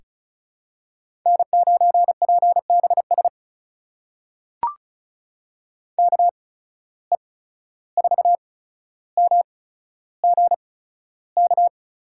Morse code, heard as a keyed tone, about 36 wpm, spaced as for 14 wpm, 700 Hz. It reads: N9PBS KE4MGK
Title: USA Call Signs - 8x Character Spacing - 35wpm